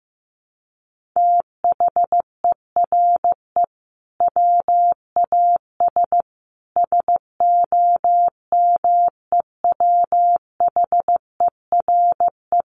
Morse code, heard as a keyed tone, about 15 wpm, 700 Hz. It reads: THERE WAS SOMEWHERE
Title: Sentences from Top 1400 Words 15wpm